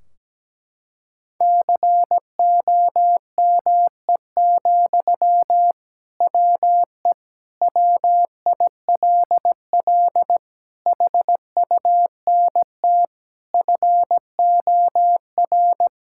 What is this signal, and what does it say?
Morse code, heard as a keyed tone, about 17 wpm, 700 Hz. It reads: COME, WE WILL HUNT FOR
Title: Sentences from Top 1600 Words 17wpm